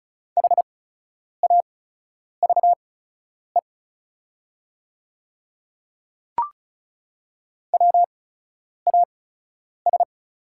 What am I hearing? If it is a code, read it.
Morse code, heard as a keyed tone, about 34 wpm, spaced as for 10 wpm, 690 Hz. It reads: HAVE WAS
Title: 8x Character Spacing - Top 300 Words - 35wpm